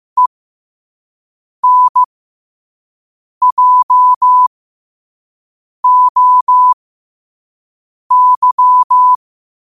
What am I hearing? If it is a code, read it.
Morse code, heard as a keyed tone, about 15 wpm, spaced as for 5 wpm, 1000 Hz. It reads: ENJOY
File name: qotdfiles / 5wpm